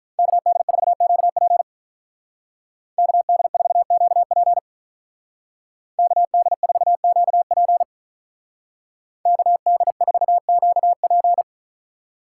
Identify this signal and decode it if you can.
Morse code, heard as a keyed tone, about 44 wpm, 700 Hz. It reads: KD4QP KD4QP KD4QP KD4QP